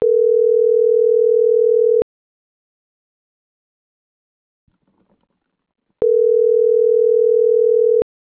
Ringback tone (North America), 440 + 480 Hz.